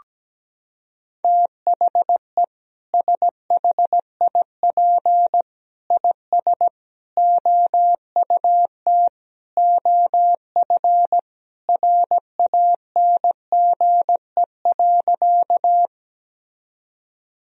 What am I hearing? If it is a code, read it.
Morse code, heard as a keyed tone, about 17 wpm, 700 Hz. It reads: THE SHIP IS OUT OF RANGE.
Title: Sentences from Top 1000 Words - Review Part 4 - 17wpm